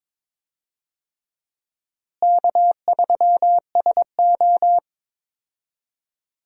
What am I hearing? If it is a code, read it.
Morse code, heard as a keyed tone, about 22 wpm, 700 Hz. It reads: K3SO
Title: USA Call Signs - 4 Characters - Repeated 3x - 22wpm